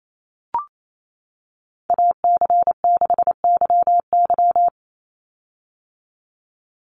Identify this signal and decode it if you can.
Morse code, heard as a keyed tone, about 28 wpm, 700 Hz. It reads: AC6YY
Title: Call Signs - 28wpm